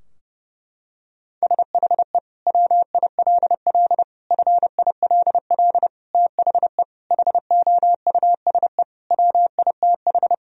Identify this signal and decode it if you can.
Morse code, heard as a keyed tone, about 30 wpm, 700 Hz. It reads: SHE WILL FILL THE HOUSE WITH
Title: Sentences from Top 1000 Words - Review Part 3 - 30wpm